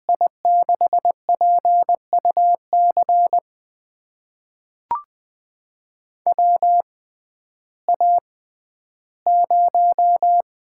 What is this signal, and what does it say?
Morse code, heard as a keyed tone, about 21 wpm, spaced as for 12 wpm, 700 Hz. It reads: I6PUC WA0
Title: USA Call Signs - 6 Characters - 6x Character Spacing - 20wpm